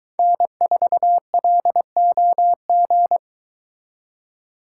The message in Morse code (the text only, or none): N4LOG